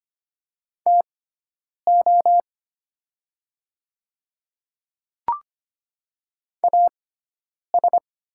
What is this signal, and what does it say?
Morse code, heard as a keyed tone, about 25 wpm, spaced as for 9 wpm, 700 Hz. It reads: TO AS
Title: 6x Character Spacing - Top 300 Words - 25wpm